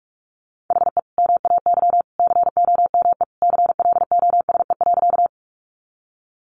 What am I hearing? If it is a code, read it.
Morse code, heard as a keyed tone, about 44 wpm, 700 Hz. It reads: HE MAY COME CLOSE.
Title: Sentences from Top 300 Words 45wpm